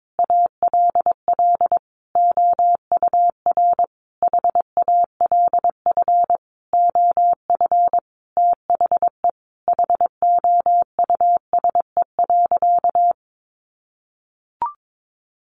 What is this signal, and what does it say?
Morse code, heard as a keyed tone, about 22 wpm, 700 Hz. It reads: ALL OUR HALF OF THE HOUSE.
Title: Sentences from Top 400 Words 22wpm